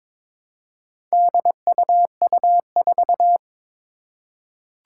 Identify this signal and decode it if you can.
Morse code, heard as a keyed tone, about 22 wpm, 700 Hz. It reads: DUU4
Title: Four Challenging Letter-Number Combinations 22wpm